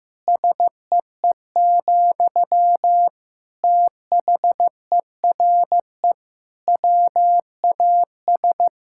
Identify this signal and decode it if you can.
Morse code, heard as a keyed tone, about 15 wpm, 700 Hz. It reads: SEE, THERE WAS
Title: Sentences from Top 1000 Words - Review Part 6 - 15wpm